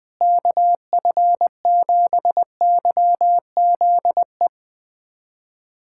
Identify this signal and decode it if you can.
Morse code, heard as a keyed tone, about 21 wpm, 700 Hz. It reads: KF7YZE
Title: USA Callsign DE Callsign 20wpm